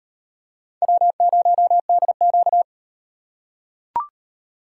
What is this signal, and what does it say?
Morse code, heard as a keyed tone, about 36 wpm, 700 Hz. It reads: W0DQ